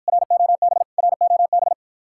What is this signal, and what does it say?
Morse code, heard as a keyed tone, about 54 wpm, 700 Hz. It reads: ROB ROB